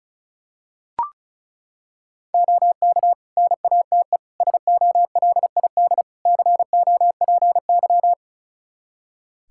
Morse code, heard as a keyed tone, about 34 wpm, 700 Hz. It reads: OK NATE SOLID COPY
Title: Practice QSO - Exchange 3 - More Info - 35wpm